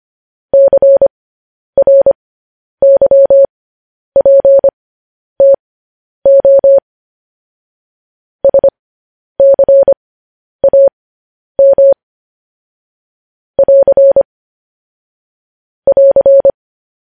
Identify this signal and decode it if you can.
Morse code, heard as a keyed tone, about 25 wpm, spaced as for 10 wpm, 550 Hz. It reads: CRYPTO SCAM <AR> <AR>
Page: News Headlines in Morse Code at 10 WPM